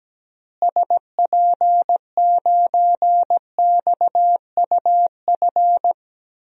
Morse code, heard as a keyed tone, about 17 wpm, 700 Hz. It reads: SP9XUF
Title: DXCC-100 Call Signs - 6 Characters - Repeated 2x - 17wpm